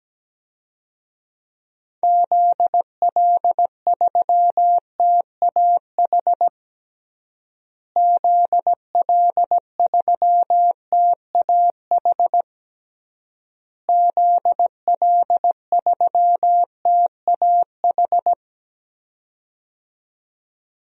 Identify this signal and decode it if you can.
Morse code, heard as a keyed tone, about 17 wpm, 700 Hz. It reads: ZL3TAH ZL3TAH ZL3TAH
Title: DXCC-100 Call Signs - 6 Characters - Repeated 3x - 17wpm